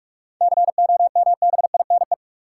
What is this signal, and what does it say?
Morse code, heard as a keyed tone, about 46 wpm, 700 Hz. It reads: COMBINE
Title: Sentences from Top 1600 Words - New Word Drill Alt - 45wpm